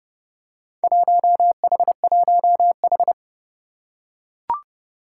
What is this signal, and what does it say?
Morse code, heard as a keyed tone, about 29 wpm, 700 Hz. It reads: 1H1H
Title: Four Challenging Letter-Number Combinations 30wpm